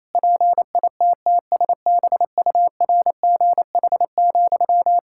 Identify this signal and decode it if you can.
Morse code, heard as a keyed tone, about 28 wpm, 700 Hz. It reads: PITTSBURGH,
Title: Practice QSO - Exchange 1 - Beginning - 28wpm